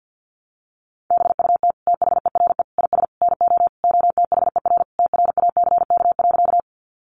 Morse code, heard as a keyed tone, about 50 wpm, 700 Hz. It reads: BUT THERE IS NO OTHER TRACK.
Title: Sentences from Top 1100 Words 50wpm